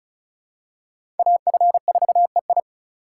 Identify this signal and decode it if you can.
Morse code, heard as a keyed tone, about 34 wpm, 700 Hz. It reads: AF4EI
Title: USA Call Signs - 5 Characters - 3x Character Spacing - 35wpm